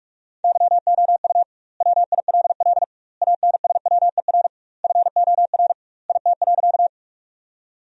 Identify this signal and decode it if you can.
Morse code, heard as a keyed tone, about 41 wpm, 700 Hz. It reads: YOU WILL ANSWER FOR IT.
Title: 2x Word-Spacing - Sentences from Top 300 Words - 45wpm